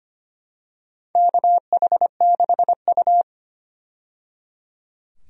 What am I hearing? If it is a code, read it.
Morse code, heard as a keyed tone, about 25 wpm, 700 Hz. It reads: KH6U
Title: DXCC-100 Call Signs - 4 Characters - 25wpm